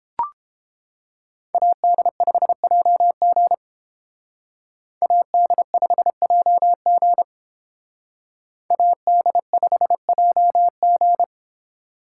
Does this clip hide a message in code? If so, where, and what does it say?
Yes: Morse code from 1.5 to 11.2 s, heard as a keyed tone, about 28 wpm, 700 Hz. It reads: AD5JG AD5JG AD5JG